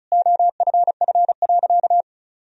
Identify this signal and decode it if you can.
Morse code, heard as a keyed tone, about 36 wpm, 700 Hz. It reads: OFF.
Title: Sentences from Top 1000 Words - Review Part 1 - 35wpm